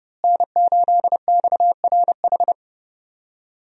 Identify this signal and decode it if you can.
Morse code, heard as a keyed tone, about 30 wpm, 700 Hz. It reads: N8XRH
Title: USA Call Signs - 5 Characters - Repeated 3x - 30wpm